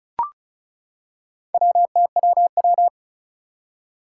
Morse code, heard as a keyed tone, about 36 wpm, 700 Hz. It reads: WTWW
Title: Four Challenging Letter-Number Combinations 35wpm